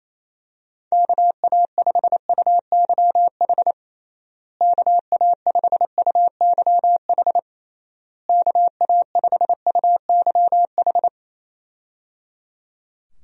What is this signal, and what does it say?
Morse code, heard as a keyed tone, about 27 wpm, 700 Hz. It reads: KA5UYH KA5UYH KA5UYH
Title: USA Call Signs - 6 Characters - Repeated 3x - 28wpm